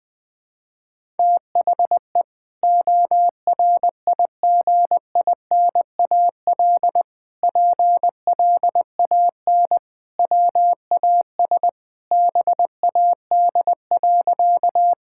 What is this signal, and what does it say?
Morse code, heard as a keyed tone, about 20 wpm, 700 Hz. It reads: THE ORIGINAL PLAN WAS BAD.
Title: Sentences from Top 1000 Words - Review Part 6 - 20wpm